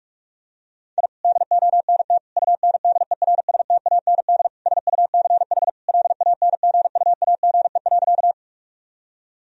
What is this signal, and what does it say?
Morse code, heard as a keyed tone, about 41 wpm, 700 Hz. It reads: I DONT UNDERSTAND SUCH LANGUAGE.
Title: Sentences from Top 1000 Words - Review Part 3 - 45wpm